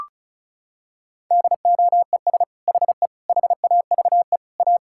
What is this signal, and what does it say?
Morse code, heard as a keyed tone, about 36 wpm, 700 Hz. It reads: DOES HE HAVE A
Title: Sentences from Top 1600 Words 35wpm